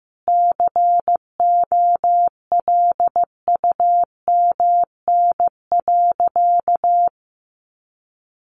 Morse code, heard as a keyed tone, about 15 wpm, 700 Hz. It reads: COLUMN.
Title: A Sound Of Thunder 15wpm